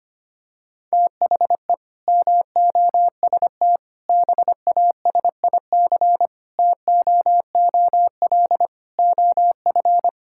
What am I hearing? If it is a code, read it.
Morse code, heard as a keyed tone, about 25 wpm, 700 Hz. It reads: THE MOST BASIC TOOL OF
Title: Sentences from Top 1000 Words - Review Part 6 - 25wpm